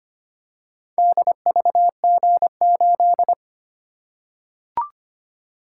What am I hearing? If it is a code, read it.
Morse code, heard as a keyed tone, about 25 wpm, 700 Hz. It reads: DVG8
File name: practice / 25wpm / Four Letter-Number Combinations - 25wpm